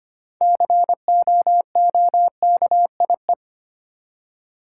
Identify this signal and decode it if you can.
Morse code, heard as a keyed tone, about 25 wpm, 690 Hz. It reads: COOKIE
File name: practice / 25wpm / Easy Words with Hints - 25wpm